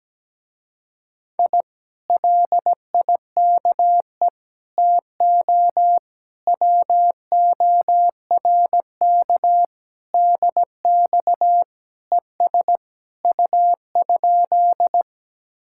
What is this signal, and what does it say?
Morse code, heard as a keyed tone, about 17 wpm, 700 Hz. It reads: I LIKE TO WORK DX ES U?
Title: Speed-Racing Common QSO Phrases - 17wpm